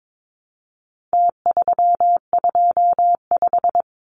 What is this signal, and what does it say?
Morse code, heard as a keyed tone, about 22 wpm, 700 Hz. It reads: T325
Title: Numbers - Contest Style - 4-Digit 22wpm